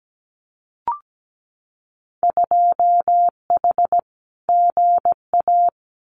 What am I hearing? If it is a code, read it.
Morse code, heard as a keyed tone, about 17 wpm, 700 Hz. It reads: 2H GA
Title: Winter Field Day Exchanges Encore 17wpm